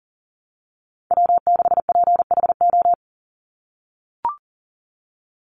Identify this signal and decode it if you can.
Morse code, heard as a keyed tone, about 40 wpm, 700 Hz. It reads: W6PHO
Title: Call Signs - 40wpm